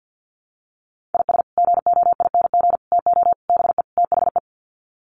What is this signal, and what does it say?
Morse code, heard as a keyed tone, about 50 wpm, 700 Hz. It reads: IS GOING TO BE THE
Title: Sets of 5 Words 50wpm